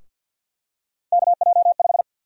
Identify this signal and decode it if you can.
Morse code, heard as a keyed tone, about 53 wpm, 700 Hz. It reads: KJ5
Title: USA Call Sign Prefixes - 50wpm